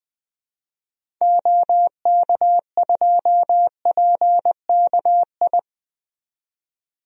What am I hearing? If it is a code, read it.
Morse code, heard as a keyed tone, about 20 wpm, 700 Hz. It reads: OK2PKI
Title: DXCC-100 Call Signs - 6 Characters - 20wpm